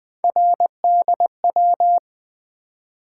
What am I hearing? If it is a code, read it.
Morse code, heard as a keyed tone, about 20 wpm, 700 Hz. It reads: RDW
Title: Three Letter-Number Combinations - 20wpm